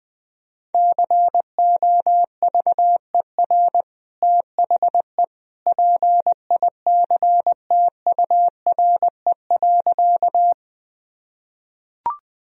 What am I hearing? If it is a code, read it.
Morse code, heard as a keyed tone, about 20 wpm, 700 Hz. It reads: COVER THE PICTURE.
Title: Sentences from Top 1000 Words - Review Part 1 - 20wpm